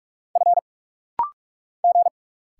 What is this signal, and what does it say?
Morse code, heard as a keyed tone, about 45 wpm, 700 Hz. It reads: F G
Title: Single Letter-Number Alt - Rapid-Fire 45wpm